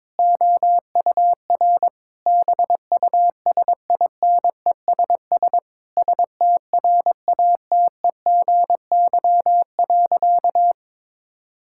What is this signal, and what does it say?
Morse code, heard as a keyed tone, about 22 wpm, 700 Hz. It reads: OUR BUSINESS STRATEGY.
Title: Sentences from Top 1000 Words - Review Part 6 - 22wpm